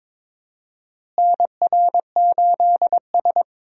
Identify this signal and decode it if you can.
Morse code, heard as a keyed tone, about 22 wpm, 700 Hz. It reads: NR8S
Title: USA Call Signs - 4 Characters - Repeated 2x - 22wpm